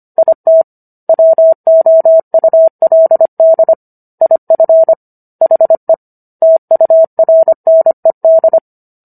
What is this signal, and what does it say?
Morse code, heard as a keyed tone, about 25 wpm, 650 Hz. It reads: IT WOULD IF HE TURNED